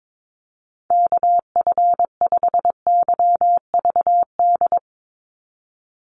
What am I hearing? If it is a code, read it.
Morse code, heard as a keyed tone, about 22 wpm, 700 Hz. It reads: KF5YVD
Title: USA Call Signs 22wpm